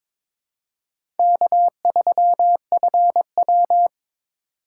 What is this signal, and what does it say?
Morse code, heard as a keyed tone, about 23 wpm, 700 Hz. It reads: K3FW
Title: USA Call Signs - 4 Characters - 4x Character Spacing - 22wpm